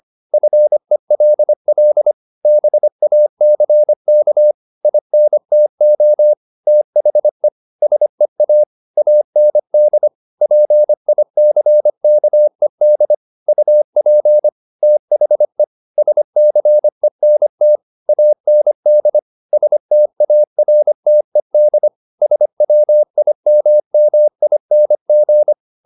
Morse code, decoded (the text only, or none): FELL BACK INTO THE SEA AND PICKED UP THE SCENT AND STARTED SWIMMING